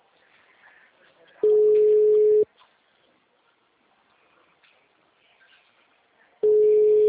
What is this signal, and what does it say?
Ringback tone (Europe, 425 Hz).
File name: out- -2002- - - .30316